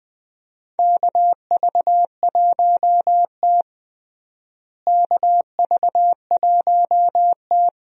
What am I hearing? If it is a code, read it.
Morse code, heard as a keyed tone, about 20 wpm, 700 Hz. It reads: KV1T KV1T
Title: USA Call Signs - 4 Characters - Repeated 3x - 20wpm